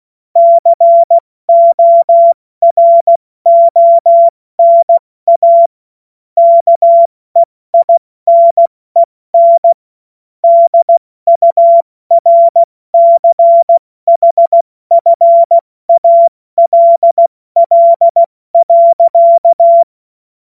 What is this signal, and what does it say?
Morse code, heard as a keyed tone, about 16 wpm, 680 Hz. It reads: CORONA KEINEN DURCHFALL.
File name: Morsekurs / Witze / Witz 29. Juni 2021, 70 F80